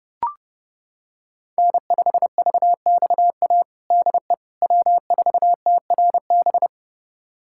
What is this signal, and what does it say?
Morse code, heard as a keyed tone, about 30 wpm, 700 Hz. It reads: N5VXA DE W4TRB
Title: USA Callsign DE Callsign - 5 Characters - 30wpm